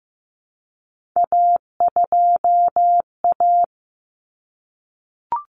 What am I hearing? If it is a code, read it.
Morse code, heard as a keyed tone, about 15 wpm, 700 Hz. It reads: A2A
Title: Common Cut Numbers - 3-Digit 15wpm